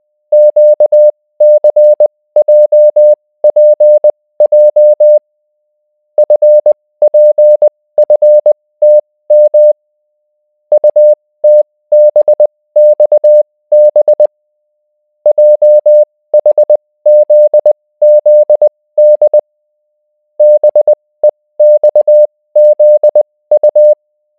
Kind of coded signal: Morse code